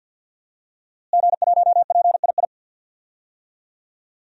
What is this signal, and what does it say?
Morse code, heard as a keyed tone, about 52 wpm, 700 Hz. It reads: G1PII
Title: DXCC-100 Call Signs - 5 Characters - Repeated 3x - 50wpm